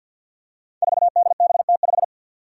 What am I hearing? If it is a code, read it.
Morse code, heard as a keyed tone, about 52 wpm, 700 Hz. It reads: 4DBT5
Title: Five Challenging Letter-Number Combinations 50wpm